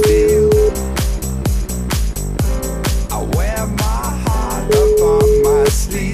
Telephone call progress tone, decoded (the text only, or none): ringback tone